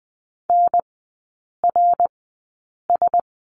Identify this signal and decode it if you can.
Morse code, heard as a keyed tone, about 20 wpm, spaced as for 8 wpm, 700 Hz. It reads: NRS
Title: 067 Lesson 12e - Letter d - Three Characters 20wpm